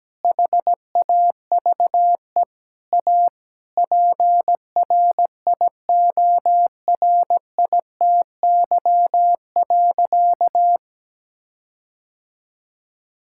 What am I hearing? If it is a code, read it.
Morse code, heard as a keyed tone, about 17 wpm, 700 Hz. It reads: HAVE A PRIORITY.
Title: Sentences from Top 1600 Words 17wpm